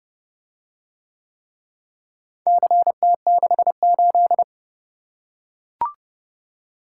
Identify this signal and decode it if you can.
Morse code, heard as a keyed tone, about 29 wpm, 700 Hz. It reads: CT68